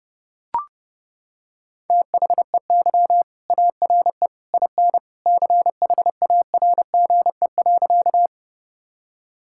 Morse code, heard as a keyed tone, about 30 wpm, 700 Hz. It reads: THEY ARE IN CHARGE.
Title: Sentences from Top 1000 Words - Review Part 6 - 30wpm